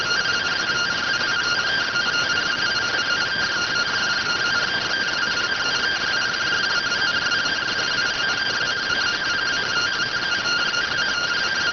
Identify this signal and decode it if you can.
Two-tone teleprinter signal, about 170 Hz shift.